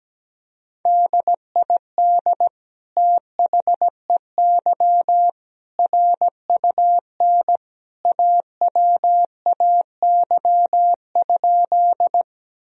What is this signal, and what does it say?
Morse code, heard as a keyed tone, about 17 wpm, 700 Hz. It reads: DID THEY RUN AWAY?
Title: Sentences from Top 1000 Words - Review Part 1 - 17wpm